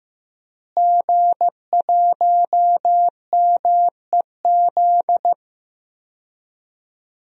Morse code, heard as a keyed tone, about 15 wpm, 700 Hz. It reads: G1MEZ
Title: DXCC-100 Call Signs - 5 Characters - Repeated 2x - 15wpm